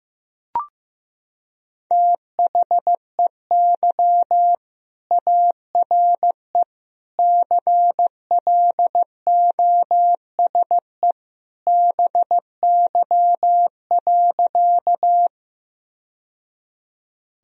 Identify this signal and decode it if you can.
Morse code, heard as a keyed tone, about 16 wpm, 700 Hz. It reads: THEY ARE CLOSE BY.
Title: Sentences from Top 1000 Words - Review Part 1 - 15wpm